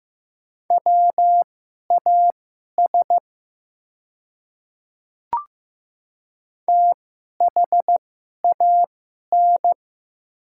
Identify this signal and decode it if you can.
Morse code, heard as a keyed tone, about 15 wpm, spaced as for 11 wpm, 700 Hz. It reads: WAS THAN
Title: 2x Character Spacing - Top 200 Words - 15wpm